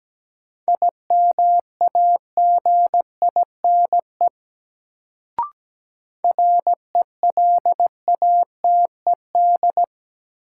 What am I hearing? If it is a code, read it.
Morse code, heard as a keyed tone, about 17 wpm, 700 Hz. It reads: IMAGINE RELATED
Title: Sentences from Top 1400 Words - New Word Drill Alt - 17wpm